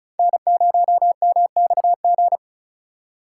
Morse code, heard as a keyed tone, about 34 wpm, 700 Hz. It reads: N0MXG
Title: USA Call Signs - 5 Characters - 3x Character Spacing - 35wpm